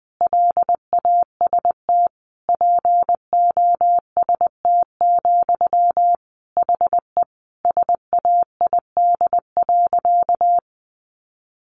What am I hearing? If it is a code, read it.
Morse code, heard as a keyed tone, about 20 wpm, 700 Hz. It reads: LAST POST, HE SAID.
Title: Sentences from Top 1000 Words 20wpm